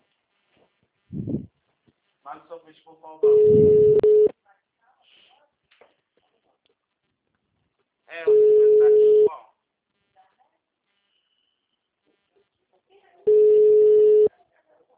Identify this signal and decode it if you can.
Ringback tone (Europe, 425 Hz), with speech.
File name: out- -2007- - - .43138